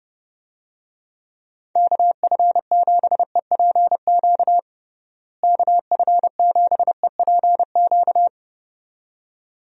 Morse code, heard as a keyed tone, about 31 wpm, 700 Hz. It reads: KF7EPQ KF7EPQ
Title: USA Call Signs - 6 Characters - Repeated 2x - 30wpm